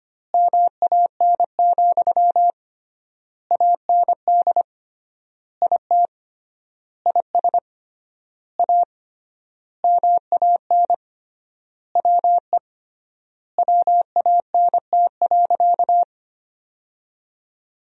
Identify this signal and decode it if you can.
Morse code, heard as a keyed tone, about 25 wpm, 700 Hz. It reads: MAN, AND IT IS A MAN WE WANT.
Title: 3x Word-Spacing - Sentences from Top 200 Words - 25wpm